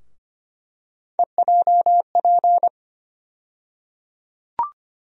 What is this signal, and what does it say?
Morse code, heard as a keyed tone, about 25 wpm, 700 Hz. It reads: EJP
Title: USA Call Sign Postfixes - 3 Letters 25wpm